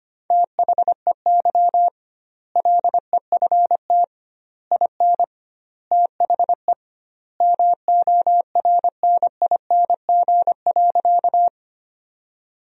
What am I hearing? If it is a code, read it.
Morse code, heard as a keyed tone, about 25 wpm, 700 Hz. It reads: THEY LEFT IN THE MORNING.
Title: 2x Word-Spacing - Sentences from Top 400 Words - 25wpm